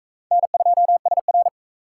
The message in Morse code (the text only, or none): N2SR